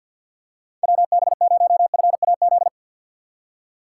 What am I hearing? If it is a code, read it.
Morse code, heard as a keyed tone, about 52 wpm, 690 Hz. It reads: WB0FAZ
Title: USA Call Signs - 2x Character Spacing - 50wpm